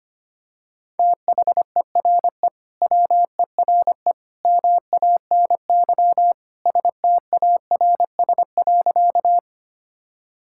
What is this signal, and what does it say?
Morse code, heard as a keyed tone, about 25 wpm, 700 Hz. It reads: THERE WERE MANY STARS.
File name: practice / 25wpm / Sentences with 2-Word Phrases - 25wpm